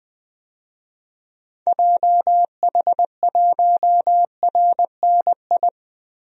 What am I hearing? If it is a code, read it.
Morse code, heard as a keyed tone, about 20 wpm, 700 Hz. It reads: JH1RNI
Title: DXCC-100 Call Signs - 6 Characters - 20wpm